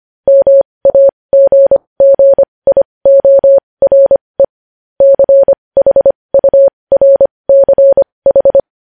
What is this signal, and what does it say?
Morse code, heard as a keyed tone, about 25 wpm, spaced as for 20 wpm, 550 Hz. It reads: MAGGIORE CHURCH
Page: news headlines in morse code at 20 wpm